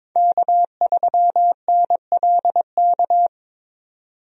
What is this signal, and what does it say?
Morse code, heard as a keyed tone, about 21 wpm, 700 Hz. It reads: K3NLK
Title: USA Call Signs - 5 Characters - 22wpm